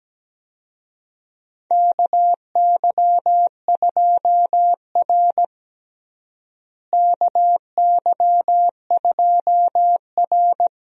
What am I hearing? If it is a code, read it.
Morse code, heard as a keyed tone, about 17 wpm, 700 Hz. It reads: KY2R KY2R
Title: USA Call Signs - 4 Characters - Repeated 3x - 17wpm